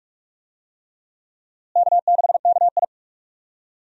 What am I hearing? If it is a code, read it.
Morse code, heard as a keyed tone, about 46 wpm, 700 Hz. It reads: K6KI
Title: USA Call Signs - 4 Characters - Repeated 3x - 45wpm